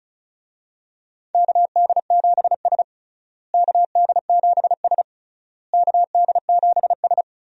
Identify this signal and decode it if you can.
Morse code, heard as a keyed tone, about 36 wpm, 700 Hz. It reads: KD7S KD7S KD7S
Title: USA Call Signs - 4 Characters - Repeated 3x - 35wpm